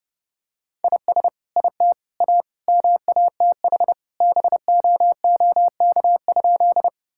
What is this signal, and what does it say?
Morse code, heard as a keyed tone, about 30 wpm, 700 Hz. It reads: IS IT A MATH BOOK?
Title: Sentences from Top 1000 Words - Review Part 6 - 30wpm